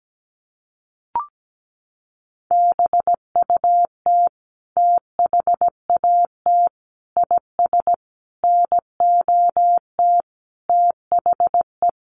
Morse code, heard as a keyed tone, about 17 wpm, 700 Hz. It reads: BUT THAT IS NOT THE
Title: Sentences from Top 600 Words 17wpm